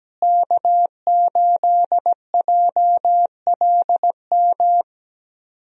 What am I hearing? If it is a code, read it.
Morse code, heard as a keyed tone, about 17 wpm, 700 Hz. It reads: K8JLM
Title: USA Call Signs - 5 Characters - Repeated 2x - 17wpm